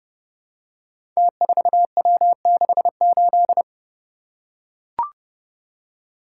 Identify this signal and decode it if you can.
Morse code, heard as a keyed tone, about 31 wpm, 700 Hz. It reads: T4W68
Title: Code Groups - 5 Letter:Numbers - 30wpm